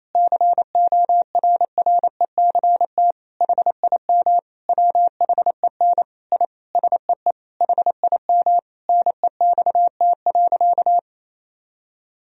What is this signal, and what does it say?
Morse code, heard as a keyed tone, about 28 wpm, 700 Hz. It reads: CORRECT HIM WHEN I SEE HIM NEXT.
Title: Sentences from Top 1000 Words - Review Part 2 - 28wpm